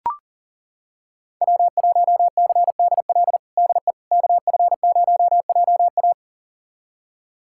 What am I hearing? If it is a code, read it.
Morse code, heard as a keyed tone, about 40 wpm, 700 Hz. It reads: W1CDL DE KF0JA
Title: USA Callsign DE Callsign - 5 Characters - 40wpm